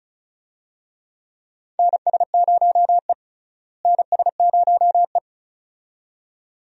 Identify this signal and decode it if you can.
Morse code, heard as a keyed tone, about 36 wpm, 700 Hz. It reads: NS0E NS0E
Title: USA Call Signs - 4 Characters - Repeated 2x - 35wpm